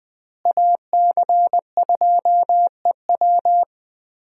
Morse code, heard as a keyed tone, about 21 wpm, 700 Hz. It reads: AC2EW
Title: USA Call Signs - 5 Characters - Repeated 3x - 20wpm